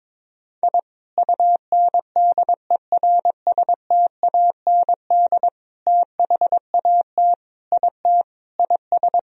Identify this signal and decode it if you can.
Morse code, heard as a keyed tone, about 22 wpm, 700 Hz. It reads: I UNDERSTAND THAT IT IS
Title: Sentences from Top 1000 Words - Review Part 1 - 22wpm